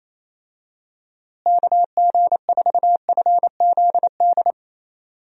Morse code, heard as a keyed tone, about 27 wpm, 700 Hz. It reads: KG4FZD
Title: USA Call Signs - 6 Characters - Repeated 3x - 28wpm